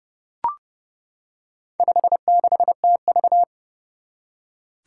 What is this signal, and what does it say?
Morse code, heard as a keyed tone, about 30 wpm, 700 Hz. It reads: 56TV
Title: Four Challenging Letter-Number Combinations 30wpm